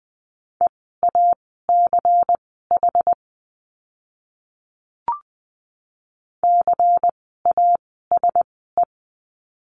Morse code, heard as a keyed tone, about 20 wpm, spaced as for 14 wpm, 700 Hz. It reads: EACH CASE
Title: 192 Lesson 31f - Letter v - Words 20wpm